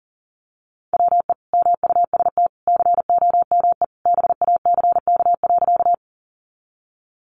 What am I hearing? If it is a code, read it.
Morse code, heard as a keyed tone, about 40 wpm, 700 Hz. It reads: WE MUST COME BACK.
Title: Sentences from Top 200 Words 40wpm